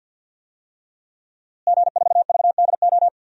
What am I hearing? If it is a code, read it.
Morse code, heard as a keyed tone, about 51 wpm, 700 Hz. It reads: G4VDO
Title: DXCC-100 Call Signs - 5 Characters - Repeated 3x - 50wpm